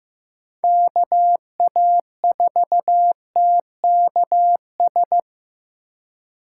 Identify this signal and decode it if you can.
Morse code, heard as a keyed tone, about 15 wpm, 700 Hz. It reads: KA4TKS
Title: USA Call Signs - 8x Character Spacing - 15wpm